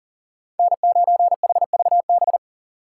Morse code, heard as a keyed tone, about 40 wpm, 700 Hz. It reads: N9HVB
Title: USA Call Signs - 5 Characters - 2x Character Spacing - 40wpm